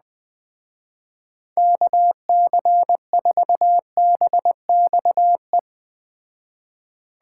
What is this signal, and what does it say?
Morse code, heard as a keyed tone, about 20 wpm, 700 Hz. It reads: KC4BXE